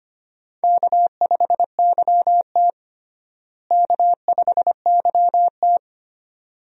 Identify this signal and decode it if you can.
Morse code, heard as a keyed tone, about 25 wpm, 700 Hz. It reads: K5YT K5YT
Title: USA Call Signs - 4 Characters - Repeated 2x - 25wpm